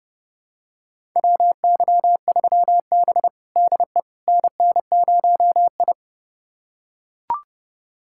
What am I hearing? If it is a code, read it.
Morse code, heard as a keyed tone, about 30 wpm, 700 Hz. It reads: WY3B DE NN0I
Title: USA Callsign DE Callsign - 4 Characters - 30wpm